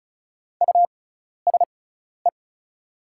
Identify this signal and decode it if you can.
Morse code, heard as a keyed tone, about 34 wpm, spaced as for 12 wpm, 700 Hz. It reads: USE
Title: 6x Character Spacing - Top 200 Words - 35wpm